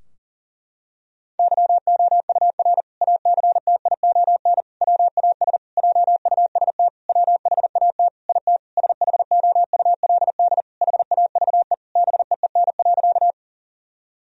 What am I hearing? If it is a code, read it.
Morse code, heard as a keyed tone, about 40 wpm, 700 Hz. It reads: YOUR ACTION WAS JUST WHAT IT SHOULD HAVE BEEN.
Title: Sentences from Top 1000 Words - Review Part 4 - 40wpm